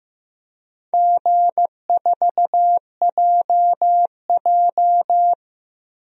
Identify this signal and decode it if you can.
Morse code, heard as a keyed tone, about 15 wpm, 700 Hz. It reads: G4JJ
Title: DXCC-100 Call Signs - 4 Characters - Repeated 3x - 15wpm